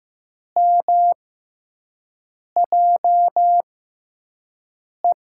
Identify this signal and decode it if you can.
Morse code, heard as a keyed tone, about 16 wpm, spaced as for 5 wpm, 700 Hz. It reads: MJE
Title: USA Call Signs - 6x Character Spacing - 15wpm